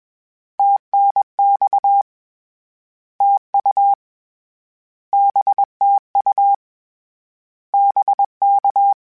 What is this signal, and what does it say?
Morse code, heard as a keyed tone, about 21 wpm, 800 Hz. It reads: TNX TU BTU BK